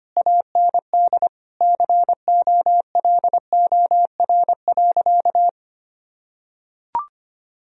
Morse code, heard as a keyed tone, about 25 wpm, 700 Hz. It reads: AND COLOR.
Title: Sentences from Top 1000 Words - Review Part 2 - 25wpm